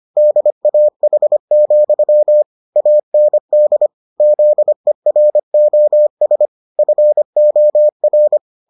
Morse code, heard as a keyed tone, about 25 wpm, 600 Hz. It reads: DAH, AND ZEROS FOR